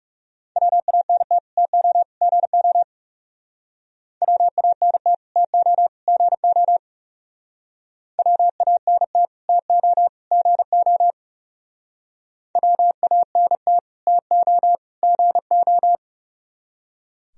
Morse code, heard as a keyed tone, about 35 wpm, 700 Hz. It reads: WANT TO GO WANT TO GO WANT TO GO WANT TO GO
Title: Speed-Racing - Sets of 3 Words 30wpm